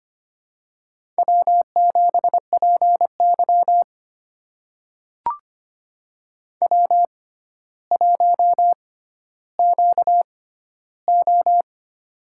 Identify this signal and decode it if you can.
Morse code, heard as a keyed tone, about 25 wpm, spaced as for 13 wpm, 700 Hz. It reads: W7PY W1QO
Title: USA Call Signs - 4 Characters - 6x Character Spacing - 25wpm